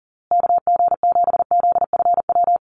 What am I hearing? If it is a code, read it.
Morse code, heard as a keyed tone, about 40 wpm, 700 Hz. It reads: KG7ZFW
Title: USA Call Signs Repeated 2x 40wpm